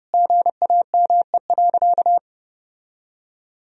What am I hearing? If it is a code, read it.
Morse code, heard as a keyed tone, about 30 wpm, 700 Hz. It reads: GAME.
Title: Sentences from Top 1000 Words - Review Part 4 - 30wpm